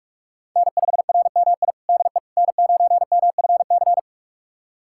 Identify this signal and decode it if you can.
Morse code, heard as a keyed tone, about 46 wpm, 700 Hz. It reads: N5RMI DE N9MFC
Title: USA Callsign DE Callsign - 5 Characters - 45wpm